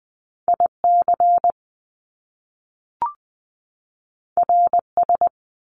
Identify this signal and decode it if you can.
Morse code, heard as a keyed tone, about 20 wpm, 700 Hz. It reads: IC RS
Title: Top Two Letter Combinations in Words 20wpm